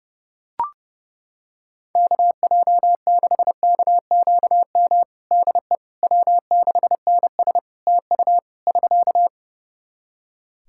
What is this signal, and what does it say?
Morse code, heard as a keyed tone, about 30 wpm, 700 Hz. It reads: KJ6KQM DE W6NS TU <SK>
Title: Practice QSO - Exchange 4 - Ending - 30wpm